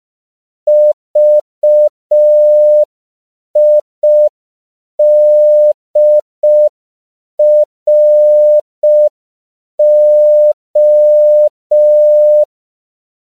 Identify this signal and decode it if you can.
Morse code, heard as a keyed tone, about 5 wpm, 600 Hz. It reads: VIDRO
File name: CW / 5PPM